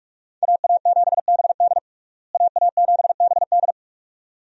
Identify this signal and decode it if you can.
Morse code, heard as a keyed tone, about 46 wpm, 700 Hz. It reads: AA7BD AA7BD
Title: USA Call Signs - 5 Characters - Repeated 2x - 45wpm